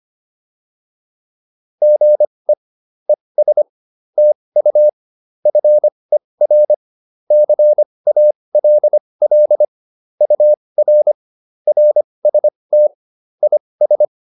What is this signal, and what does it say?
Morse code, heard as a keyed tone, about 25 wpm, spaced as for 20 wpm, 600 Hz. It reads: GE ES TU FER CALL UR RST IS